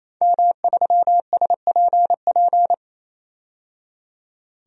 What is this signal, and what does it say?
Morse code, heard as a keyed tone, about 27 wpm, 700 Hz. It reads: M3SPP
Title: DXCC-100 Call Signs - 5 Characters - 28wpm